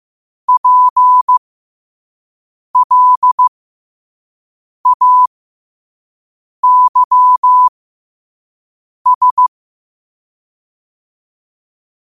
Morse code, heard as a keyed tone, about 15 wpm, spaced as for 5 wpm, 1000 Hz. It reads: PLAYS
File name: qotdfiles / 5wpm